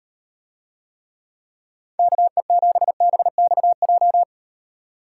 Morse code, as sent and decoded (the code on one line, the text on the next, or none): -.- . --... -... -..- .---
KE7BXJ